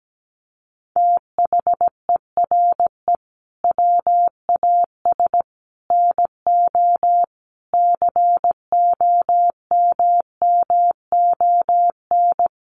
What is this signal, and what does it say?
Morse code, heard as a keyed tone, about 17 wpm, 700 Hz. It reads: THERE WAS NO COMMON
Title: Sentences from Top 1200 Words 17wpm